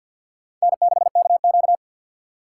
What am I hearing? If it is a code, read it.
Morse code, heard as a keyed tone, about 48 wpm, 700 Hz. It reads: N6KQ
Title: USA Call Signs - 2x Character Spacing - 50wpm